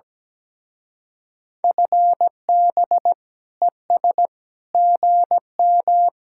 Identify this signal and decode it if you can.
Morse code, heard as a keyed tone, about 19 wpm, 700 Hz. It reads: FB ES GM
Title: Speed-Racing Common QSO Phrases Encore - 17wpm